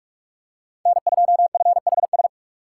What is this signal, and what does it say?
Morse code, heard as a keyed tone, about 41 wpm, 700 Hz. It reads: N2FHS
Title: USA Call Signs - 5 Characters - 4x Character Spacing - 45wpm